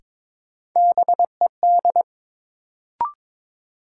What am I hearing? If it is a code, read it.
Morse code, heard as a keyed tone, about 21 wpm, 700 Hz. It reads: BED
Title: Sentences from Top 500 Words - New Word Drill - 22wpm